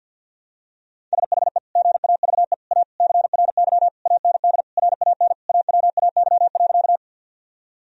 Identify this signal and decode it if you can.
Morse code, heard as a keyed tone, about 50 wpm, 700 Hz. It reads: SHE GAVE A CRY AND RAN AWAY.
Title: Sentences from Top 1000 Words - Review Part 3 - 50wpm